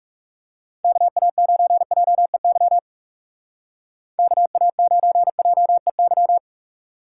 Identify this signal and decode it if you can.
Morse code, heard as a keyed tone, about 40 wpm, 700 Hz. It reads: KA9JEY KA9JEY